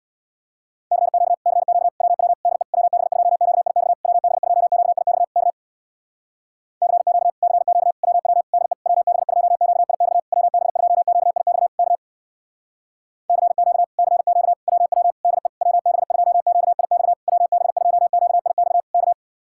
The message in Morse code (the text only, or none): CQ CQ WW DE WB2QIX WB2QIX K CQ CQ WW DE WB2QIX WB2QIX K CQ CQ WW DE WB2QIX WB2QIX K